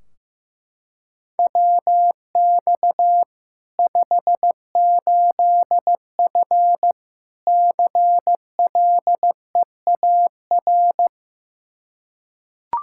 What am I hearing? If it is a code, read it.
Morse code, heard as a keyed tone, about 15 wpm, 700 Hz. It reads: WX 58F CLEAR
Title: Practice QSO - Exchange 2 - Info - 15wpm